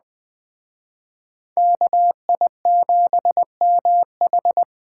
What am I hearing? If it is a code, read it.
Morse code, heard as a keyed tone, about 20 wpm, 700 Hz. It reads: KI7MH